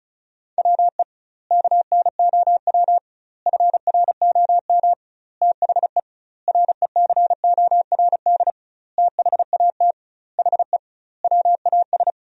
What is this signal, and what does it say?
Morse code, heard as a keyed tone, about 34 wpm, 700 Hz. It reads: WE KNOW FROM THE RECORD THAT HE WAS
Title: 2x Word-Spacing - Sentences from Top 500 Words - 35wpm